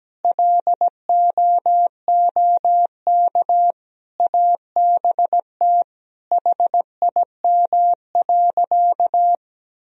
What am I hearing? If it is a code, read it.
Morse code, heard as a keyed tone, about 17 wpm, 700 Hz. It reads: LOOK ABT HIM.
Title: Sentences from Top 200 Words and Abbreviations 17wpm